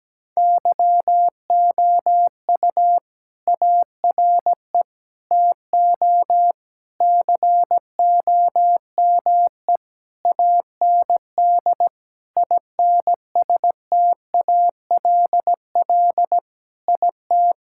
Morse code, heard as a keyed tone, about 17 wpm, 700 Hz. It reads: YOU ARE TO COME AND INSTALL IT
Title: Sentences from Top 1600 Words 17wpm